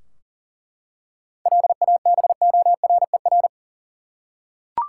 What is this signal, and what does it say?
Morse code, heard as a keyed tone, about 40 wpm, 700 Hz. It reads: LABORER
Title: K1USN SST QSO Practice 40wpm